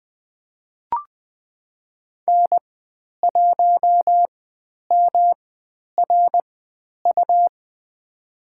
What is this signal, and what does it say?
Morse code, heard as a keyed tone, about 20 wpm, spaced as for 10 wpm, 700 Hz. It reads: N1MRU
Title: 109 Lesson 18g - Letter m - Call Signs 20wpm